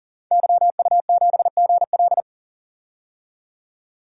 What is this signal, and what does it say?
Morse code, heard as a keyed tone, about 40 wpm, 700 Hz. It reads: YU7GL
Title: DXCC-100 Call Signs - 5 Characters - 40wpm